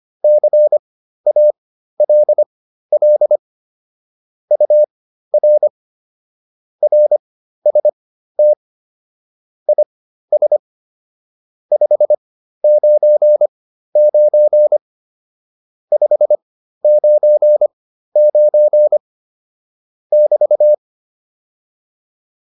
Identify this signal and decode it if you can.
Morse code, heard as a keyed tone, about 25 wpm, spaced as for 13 wpm, 600 Hz. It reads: CALL UR RST IS 599 599 <BT>
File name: QSO 108 13